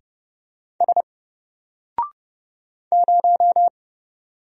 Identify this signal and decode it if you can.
Morse code, heard as a keyed tone, about 30 wpm, 700 Hz. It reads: S 0
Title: Single Letter-Number 30wpm